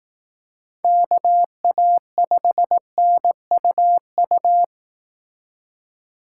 Morse code, heard as a keyed tone, about 18 wpm, 700 Hz. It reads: KA5NUU
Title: Speed Racing - USA Call Signs 15wpm